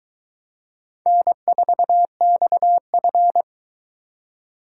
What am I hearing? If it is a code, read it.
Morse code, heard as a keyed tone, about 21 wpm, 700 Hz. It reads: N4XF